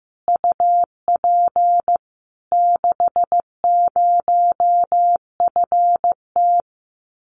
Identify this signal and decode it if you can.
Morse code, heard as a keyed tone, about 15 wpm, 700 Hz. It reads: UP 60FT
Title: Common QSO Phrases Encore 15wpm